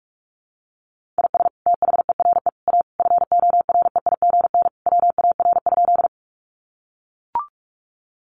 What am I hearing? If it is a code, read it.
Morse code, heard as a keyed tone, about 44 wpm, 700 Hz. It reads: IS THERE A FOREIGN WAR?
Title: Sentences from Top 1400 Words 45wpm